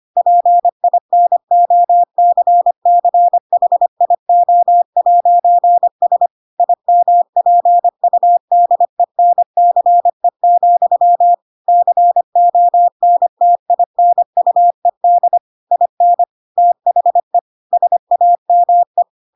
Morse code, heard as a keyed tone, about 25 wpm, 700 Hz. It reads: PINOCCHIO'S IMPUDENCE, CONTINUED IN THE SAME
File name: Pinocchio / Pinocchio ch4 25 25 ik1hge